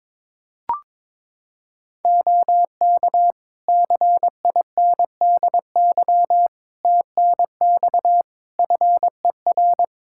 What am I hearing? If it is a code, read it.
Morse code, heard as a keyed tone, about 22 wpm, 700 Hz. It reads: OK CINDY TNX FER
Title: K1USN SST QSO Practice 22wpm